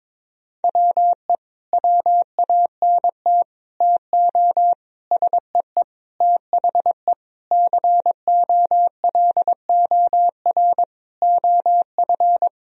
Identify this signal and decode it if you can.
Morse code, heard as a keyed tone, about 22 wpm, 700 Hz. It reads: WE WANT TO SEE THE COLOR OF
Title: Sentences from Top 1000 Words - Review Part 2 - 22wpm